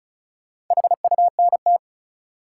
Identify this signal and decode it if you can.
Morse code, heard as a keyed tone, about 36 wpm, 700 Hz. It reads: HUNT
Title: Sentences from Top 1600 Words - New Word Drill - 35wpm